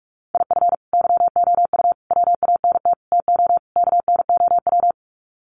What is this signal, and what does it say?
Morse code, heard as a keyed tone, about 44 wpm, 700 Hz. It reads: IF YOU WANT TO KNOW
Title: Sets of 5 Words Encore2 45wpm